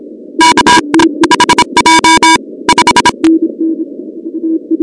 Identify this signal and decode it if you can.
Morse code, heard as a keyed tone, about 26 wpm, 970 Hz, with noise. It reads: K HJ 5
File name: contests / 2019-cw-wpx-kt5j-k5pi- -2330 0-38